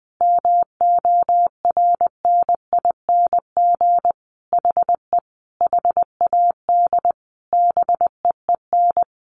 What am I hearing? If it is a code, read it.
Morse code, heard as a keyed tone, about 20 wpm, 700 Hz. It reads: MORNING HE HAD BEEN
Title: Sentences from Top 1200 Words 20wpm